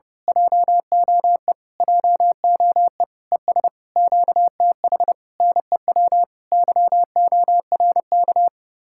Morse code, heard as a keyed tone, about 28 wpm, 700 Hz. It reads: JOE JOE ES QTH NEW YORK
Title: Speed-Racing Common QSO Phrases Encore - 30wpm